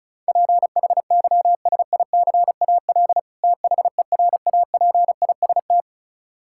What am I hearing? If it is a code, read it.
Morse code, heard as a keyed tone, about 36 wpm, 700 Hz. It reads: PHYSICAL THERAPIST
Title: Practice QSO - Exchange 3 - More Info - 35wpm